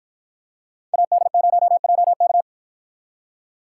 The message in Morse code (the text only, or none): AD0JK